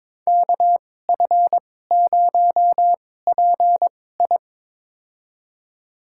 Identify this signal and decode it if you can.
Morse code, heard as a keyed tone, about 22 wpm, spaced as for 16 wpm, 700 Hz. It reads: KF0PI
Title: USA Call Signs - 5 Characters - 2x Character Spacing - 22wpm